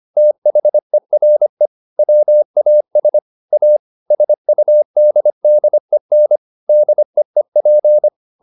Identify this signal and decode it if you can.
Morse code, heard as a keyed tone, about 25 wpm, 600 Hz. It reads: THERE WAS A SUDDEN DEEP
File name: the old man and the sea part 1 25wpm